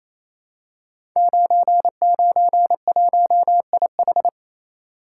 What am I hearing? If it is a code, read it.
Morse code, heard as a keyed tone, about 28 wpm, 700 Hz. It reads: 991IH
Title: Code Groups - 5 Letter:Numbers - 28wpm